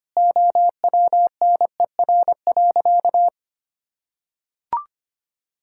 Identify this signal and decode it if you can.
Morse code, heard as a keyed tone, about 25 wpm, 700 Hz. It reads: OWNER.
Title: Sentences from Top 1600 Words 25wpm